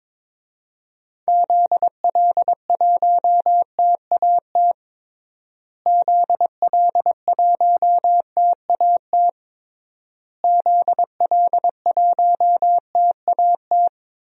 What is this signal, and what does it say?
Morse code, heard as a keyed tone, about 22 wpm, 700 Hz. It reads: ZL1TAT ZL1TAT ZL1TAT
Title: DXCC-100 Call Signs - 6 Characters - Repeated 3x - 22wpm